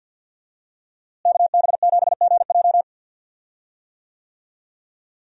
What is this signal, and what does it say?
Morse code, heard as a keyed tone, about 49 wpm, 700 Hz. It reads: KB7GJ
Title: USA Call Signs - 5 Characters - Repeated 3x - 50wpm